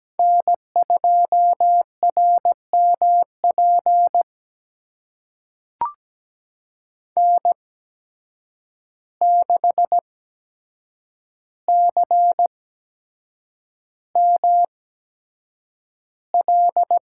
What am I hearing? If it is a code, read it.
Morse code, heard as a keyed tone, about 17 wpm, spaced as for 7 wpm, 700 Hz. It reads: N2RMP N6CML
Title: USA Call Signs - 8x Character Spacing - 17wpm